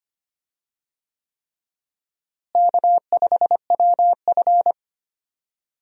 Morse code, heard as a keyed tone, about 25 wpm, 700 Hz. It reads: K5WF
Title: USA Call Signs - 4 Characters - Repeated 3x - 25wpm